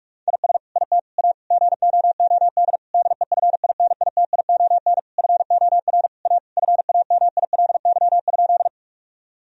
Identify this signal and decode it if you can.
Morse code, heard as a keyed tone, about 42 wpm, 700 Hz. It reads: IS IT A GOOD DEFINITION FOR A FAMILY?
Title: Sentences from Top 1000 Words - Review Part 4 - 45wpm